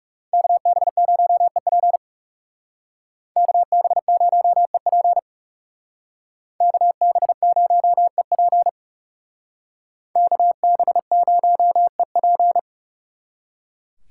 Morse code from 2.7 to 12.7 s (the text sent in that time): KB0EP KB0EP KB0EP